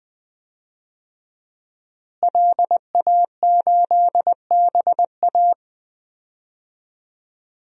Morse code, heard as a keyed tone, about 20 wpm, 700 Hz. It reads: LA8BA
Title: DXCC-100 Call Signs - 5 Characters - Repeated 2x - 20wpm